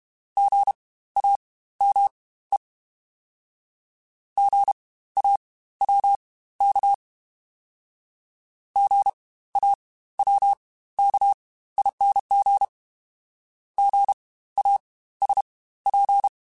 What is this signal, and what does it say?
Morse code, heard as a keyed tone, about 32 wpm, spaced as for 16 wpm, 800 Hz. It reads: GAME GAWK GAWKING GASP
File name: Spaced Alphas / Spaced Alphas mp3s / Gs sp @30